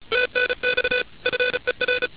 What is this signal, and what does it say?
Morse code, heard as a keyed tone, about 33 wpm, 500 Hz, with noise. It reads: TNX FER